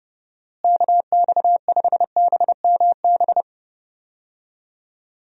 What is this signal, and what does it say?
Morse code, heard as a keyed tone, about 32 wpm, 700 Hz. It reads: KX5BMB
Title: USA Call Signs - 6 Characters - Repeated 2x - 30wpm